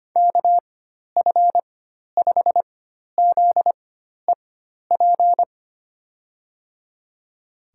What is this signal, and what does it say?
Morse code, heard as a keyed tone, about 25 wpm, spaced as for 12 wpm, 700 Hz. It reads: KF5ZEP
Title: USA Call Signs - 6 Characters - 4x Character Spacing - 25wpm